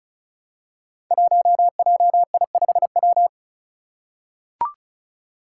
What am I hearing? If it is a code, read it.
Morse code, heard as a keyed tone, about 36 wpm, 700 Hz. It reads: 1JI5W
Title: Five Challenging Letter-Number Combinations 35wpm